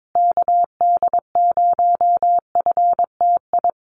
Morse code, heard as a keyed tone, about 22 wpm, 700 Hz. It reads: KD0FTI
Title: USA Call Signs Repeated 3x 22wpm